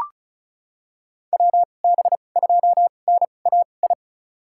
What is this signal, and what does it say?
Morse code, heard as a keyed tone, about 36 wpm, spaced as for 25 wpm, 700 Hz. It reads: WB2NAI
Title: USA Call Signs - 6 Characters - 2x Character Spacing - 35wpm